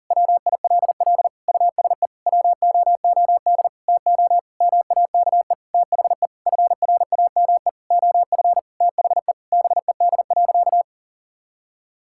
Morse code, heard as a keyed tone, about 40 wpm, 700 Hz. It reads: WILL USE WOOD TO MAKE THE FRAME OF THE BED.
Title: Sentences from Top 1000 Words - Review Part 6 - 40wpm